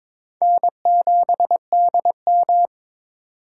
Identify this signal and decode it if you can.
Morse code, heard as a keyed tone, about 22 wpm, 700 Hz. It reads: N7DM
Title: USA Call Signs - 4 Characters - 8x Character Spacing - 22wpm